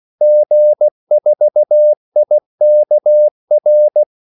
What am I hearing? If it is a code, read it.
Morse code, heard as a keyed tone, about 16 wpm, 600 Hz. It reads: G4IKR
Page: Kaldesignaler 16 wpm